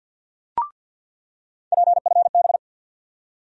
Morse code, heard as a keyed tone, about 48 wpm, 700 Hz. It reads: PFB
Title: USA Call Sign Postfixes - 3 Letters 50wpm